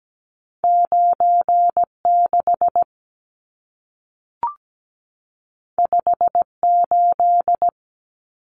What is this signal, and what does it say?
Morse code, heard as a keyed tone, about 17 wpm, 700 Hz. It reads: 96 58
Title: Numbers - 2-Digit 17wpm